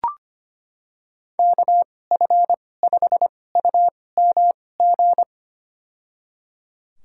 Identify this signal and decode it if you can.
Morse code, heard as a keyed tone, about 25 wpm, spaced as for 18 wpm, 700 Hz. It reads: KF5UMG
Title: USA Call Signs - 6 Characters - 2x Character Spacing - 25wpm